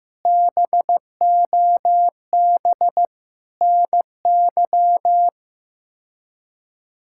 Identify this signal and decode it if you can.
Morse code, heard as a keyed tone, about 15 wpm, 700 Hz. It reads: BOB NY
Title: K1USN SST QSO Practice - 15wpm